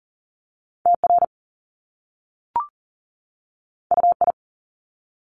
Morse code, heard as a keyed tone, about 38 wpm, 700 Hz. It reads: TR UI
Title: Top Two Letter Combinations in Words 40wpm